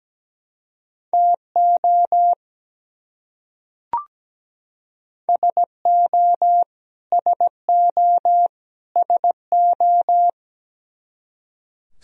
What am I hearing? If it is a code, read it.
Morse code, heard as a keyed tone, about 17 wpm, 700 Hz. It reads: TO SO SO SO
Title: Top 100 Words Repeated 17wpm